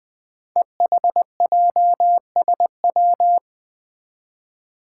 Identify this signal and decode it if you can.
Morse code, heard as a keyed tone, about 20 wpm, 700 Hz. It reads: EHJSW
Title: Code Groups - 5 Letters - 20wpm